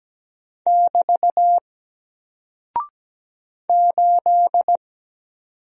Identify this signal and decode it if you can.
Morse code, heard as a keyed tone, about 17 wpm, 700 Hz. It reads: <BT> 8
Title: Single Character (Letter:Number:Punctuation:Prosign) 17wpm